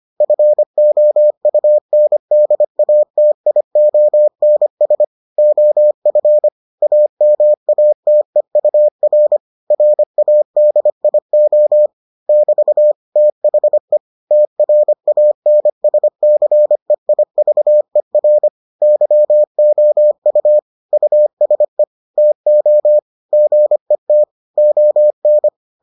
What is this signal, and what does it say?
Morse code, heard as a keyed tone, about 25 wpm, 600 Hz. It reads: FOUNDATIONS OF AMATEUR RADIO <BT> THE TRANSCEIVER YOU USE TO GET ON